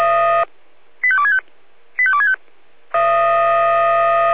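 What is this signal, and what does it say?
Five-tone selective call (ZVEI), heard as quick runs of five tones, tones of about 70 ms. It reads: 85265 85265
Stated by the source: ZVEI five-tone selective call received by SDR.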